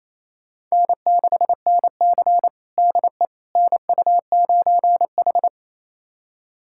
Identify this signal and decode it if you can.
Morse code, heard as a keyed tone, about 28 wpm, 700 Hz. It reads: N6NC DE NU9H
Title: USA Callsign DE Callsign - 4 Characters - 28wpm